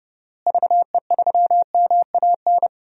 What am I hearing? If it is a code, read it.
Morse code, heard as a keyed tone, about 30 wpm, 700 Hz. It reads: VE3MAN
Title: DXCC-100 Call Signs - 6 Characters - Repeated 2x - 30wpm